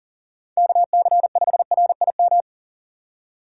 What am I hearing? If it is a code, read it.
Morse code, heard as a keyed tone, about 40 wpm, 700 Hz. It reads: KC5RIM
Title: USA Call Signs - 6x Character Spacing - 40wpm